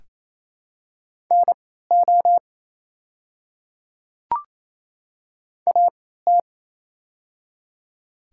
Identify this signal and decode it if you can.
Morse code, heard as a keyed tone, about 28 wpm, spaced as for 16 wpm, 700 Hz. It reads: NO AT
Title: 3x Character Spacing - Top 100 Words - 28wpm